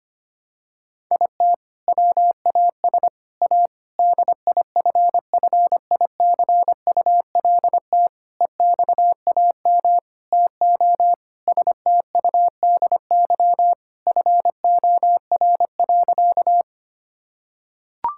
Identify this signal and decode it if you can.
Morse code, heard as a keyed tone, about 25 wpm, 700 Hz. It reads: IT WAS A DIFFICULT EXAM TO STUDY FOR.
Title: Sentences from Top 1000 Words - Review Part 5 - 25wpm